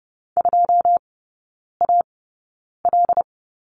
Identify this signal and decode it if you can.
Morse code, heard as a keyed tone, about 30 wpm, spaced as for 9 wpm, 700 Hz. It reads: 2AL
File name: Lesson 19e - Letter w - Three Characters 30wpm